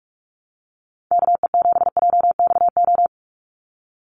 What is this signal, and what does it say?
Morse code, heard as a keyed tone, about 44 wpm, 700 Hz. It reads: KE7JXO
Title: USA Call Signs 45wpm